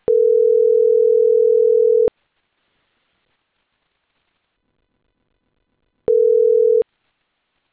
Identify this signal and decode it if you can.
Ringback tone (North America), 440 + 480 Hz.